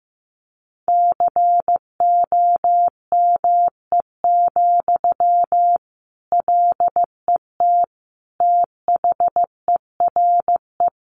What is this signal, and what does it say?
Morse code, heard as a keyed tone, about 15 wpm, 700 Hz. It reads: COME, LET THERE
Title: Sentences from Top 1500 Words 15wpm